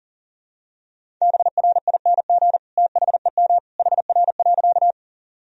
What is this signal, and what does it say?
Morse code, heard as a keyed tone, about 40 wpm, 700 Hz. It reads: BRING THEM HR.
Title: Sentences from Top 500 Words and Abbreviations 40wpm